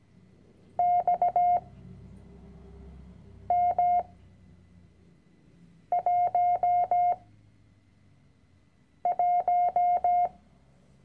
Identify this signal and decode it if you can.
Morse code, heard as a keyed tone, about 17 wpm, 710 Hz. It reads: X M 1 1
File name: CW-Online-Kurs / Anfänger / Lektion 010